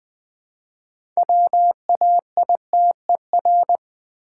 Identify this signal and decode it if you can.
Morse code, heard as a keyed tone, about 20 wpm, 700 Hz. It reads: WAITER
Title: Build A Word - 2 Letters At A Time - 6-Letter Words - 20wpm